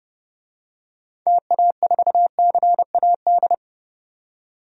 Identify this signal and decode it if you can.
Morse code, heard as a keyed tone, about 30 wpm, 700 Hz. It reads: TA4CAD
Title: DXCC-100 Call Signs - 6 Characters - 30wpm